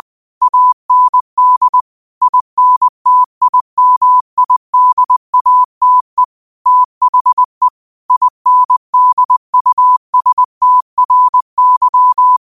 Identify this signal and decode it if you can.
Morse code, heard as a keyed tone, about 20 wpm, 1000 Hz. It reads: AND INTIMIDATE THE INDUSTRY
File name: qotdfiles / 20wpm